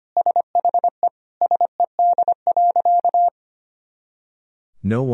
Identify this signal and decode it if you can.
Morse code, heard as a keyed tone, about 25 wpm, 700 Hz. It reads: SHE SED.
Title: Sentences from Top 100 Words and Abbreviations 25wpm